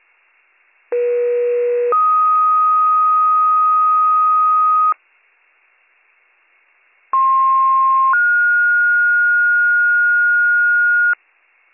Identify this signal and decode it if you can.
Two-tone page, heard as a one-second tone, then a three-second tone, about 480 then 1250 Hz, about 1060 then 1430 Hz. It is heard from 0.9 to 11.1 s.